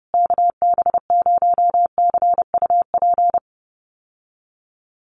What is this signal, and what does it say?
Morse code, heard as a keyed tone, about 30 wpm, 700 Hz. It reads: KB0CUP
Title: USA Call Signs Repeated 3x 30wpm